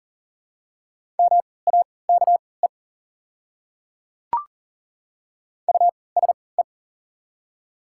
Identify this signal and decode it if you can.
Morse code, heard as a keyed tone, about 37 wpm, spaced as for 22 wpm, 700 Hz. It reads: MAKE USE
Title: 3x Character Spacing - Top 200 Words - 40wpm